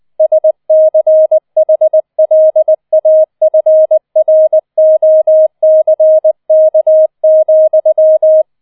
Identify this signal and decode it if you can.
Morse code, heard as a keyed tone, about 20 wpm, 630 Hz. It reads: SCHLAFROCK,